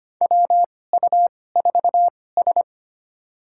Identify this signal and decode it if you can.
Morse code, heard as a keyed tone, about 25 wpm, spaced as for 18 wpm, 700 Hz. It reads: WU4S
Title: USA Call Signs - 4 Characters - 2x Character Spacing - 25wpm